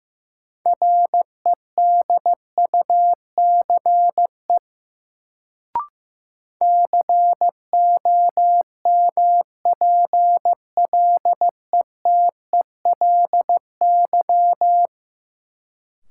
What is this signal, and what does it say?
Morse code, heard as a keyed tone, about 16 wpm, 700 Hz. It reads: REDUCE COMPLETELY
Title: Sentences from Top 1100 Words - New Word Drill - 15wpm